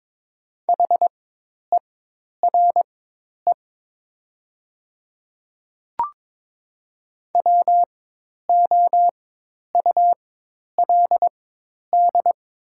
Morse code, heard as a keyed tone, about 22 wpm, spaced as for 10 wpm, 700 Hz. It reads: HERE WOULD
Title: 4x Character Spacing - Top 200 Words - 22wpm